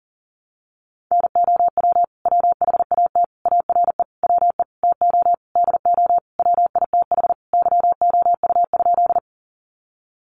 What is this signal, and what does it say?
Morse code, heard as a keyed tone, about 40 wpm, 700 Hz. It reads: NOW WHAT ARE WE TO DO WITH YOU?
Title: Sentences from Top 100 Words 40wpm